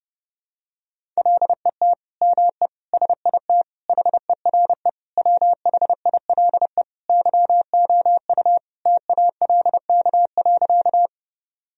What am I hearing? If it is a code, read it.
Morse code, heard as a keyed tone, about 30 wpm, 700 Hz. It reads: LET ME SIT HERE WHILE YOU TALK.
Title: Sentences from Top 1000 Words - Review Part 1 - 30wpm